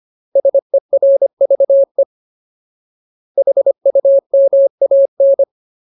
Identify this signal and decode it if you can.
Morse code, heard as a keyed tone, about 25 wpm, 550 Hz. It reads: SERVE HUMAN
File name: Ultimate Word List / 25WPM Ultimate